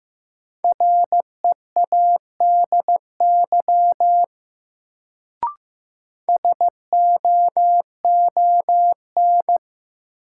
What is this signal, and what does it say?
Morse code, heard as a keyed tone, about 15 wpm, 700 Hz. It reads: READY SOON
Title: Sentences from Top 400 Words - New Word Drill - 15wpm